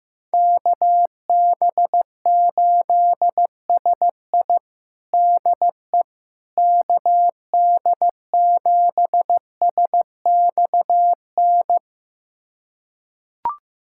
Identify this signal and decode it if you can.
Morse code, heard as a keyed tone, about 16 wpm, 700 Hz. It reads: KB8SI DE KD7SXN
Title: USA Callsign DE Callsign 15wpm